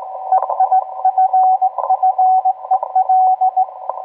Morse code, heard as a keyed tone, about 19 wpm, 750 Hz, with radio noise. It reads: EIFFEL